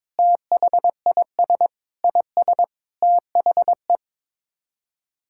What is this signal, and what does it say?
Morse code, heard as a keyed tone, about 22 wpm, 700 Hz. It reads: THIS IS THE
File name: Sets of 9 Words - 22wpm